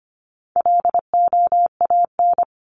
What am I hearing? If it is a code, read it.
Morse code, heard as a keyed tone, about 25 wpm, 700 Hz. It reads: LOAN
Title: Top 1501-1750 Words 25wpm